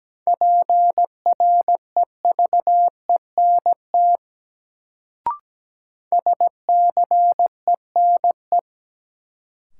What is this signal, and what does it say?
Morse code, heard as a keyed tone, about 17 wpm, 700 Hz. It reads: PREVENT SCENE
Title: Sentences from Top 1100 Words - New Word Drill Alt - 17wpm